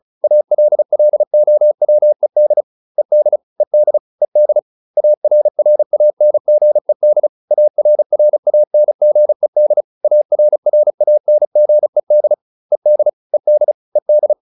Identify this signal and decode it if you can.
Morse code, heard as a keyed tone, about 34 wpm, 600 Hz. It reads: ALLOWED ED ED ED ARRANGED ARRANGED ARRANGED ED ED ED